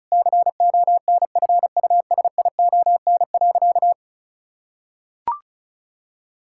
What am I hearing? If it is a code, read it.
Morse code, heard as a keyed tone, about 36 wpm, 700 Hz. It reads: CONFUSION.
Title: Sentences from Top 1600 Words 35wpm